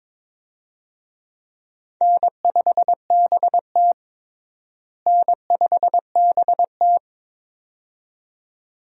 Morse code, heard as a keyed tone, about 22 wpm, 700 Hz. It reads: N5BT N5BT
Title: USA Call Signs - 4 Characters - Repeated 2x - 22wpm